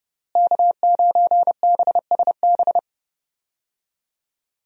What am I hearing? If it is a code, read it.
Morse code, heard as a keyed tone, about 32 wpm, 700 Hz. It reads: K9BSB
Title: USA Call Signs - 5 Characters - Repeated 3x - 30wpm